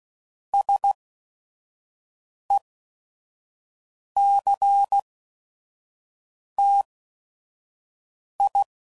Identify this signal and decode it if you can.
Morse code, heard as a keyed tone, about 16 wpm, spaced as for 5 wpm, 800 Hz. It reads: SECTI
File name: CW Code Groups @5WPM